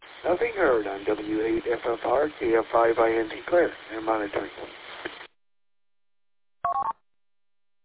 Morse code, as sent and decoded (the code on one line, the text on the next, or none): none